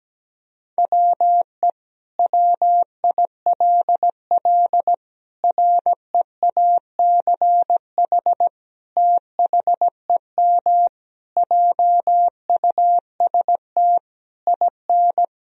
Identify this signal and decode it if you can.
Morse code, heard as a keyed tone, about 17 wpm, 700 Hz. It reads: WE WILL REACH THEM JUST IN
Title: Sentences from Top 1000 Words - Review Part 2 - 17wpm